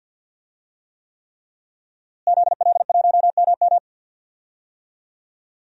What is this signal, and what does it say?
Morse code, heard as a keyed tone, about 52 wpm, 700 Hz. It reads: ZL1MM
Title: DXCC-100 Call Signs - 5 Characters - Repeated 3x - 50wpm